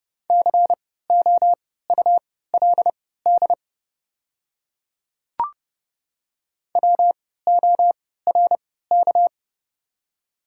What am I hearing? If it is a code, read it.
Morse code, heard as a keyed tone, about 30 wpm, spaced as for 17 wpm, 690 Hz. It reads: COULD WORK
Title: 3x Character Spacing - Top 300 Words - 30wpm